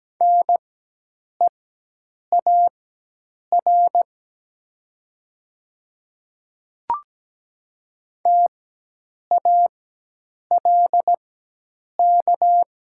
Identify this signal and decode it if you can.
Morse code, heard as a keyed tone, about 17 wpm, spaced as for 8 wpm, 700 Hz. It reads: NEAR TALK
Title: 4x Character Spacing - Top 500 Words - 17wpm